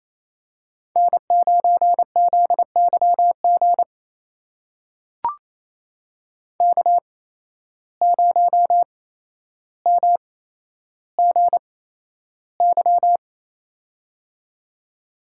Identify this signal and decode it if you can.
Morse code, heard as a keyed tone, about 27 wpm, spaced as for 12 wpm, 700 Hz. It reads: N9ZYG K0MGY
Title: USA Call Signs - 5 Characters - 8x Character Spacing - 28wpm